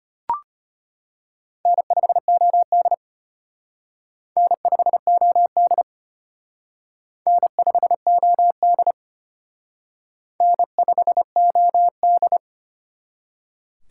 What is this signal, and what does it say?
Morse code, heard as a keyed tone, about 34 wpm, 700 Hz. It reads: N5OD N5OD N5OD N5OD